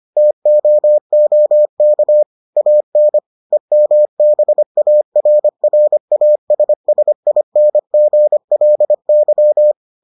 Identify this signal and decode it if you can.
Morse code, heard as a keyed tone, about 25 wpm, 600 Hz. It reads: TOOK AN EMBARRASSINGLY